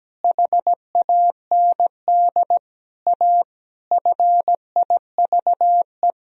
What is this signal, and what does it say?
Morse code, heard as a keyed tone, about 17 wpm, 700 Hz. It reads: HAND A FIVE
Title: Sentences from Top 1000 Words - Review Part 2 - 17wpm